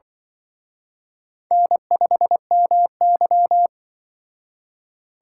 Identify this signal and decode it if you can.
Morse code, heard as a keyed tone, about 24 wpm, 700 Hz. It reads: N5MY
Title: Speed Racing - USA Call Signs 20wpm